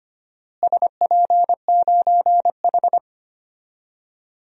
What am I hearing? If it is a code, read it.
Morse code, heard as a keyed tone, about 25 wpm, 700 Hz. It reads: SP9H
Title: DXCC-100 Call Signs - 4 Characters - Repeated 3x - 25wpm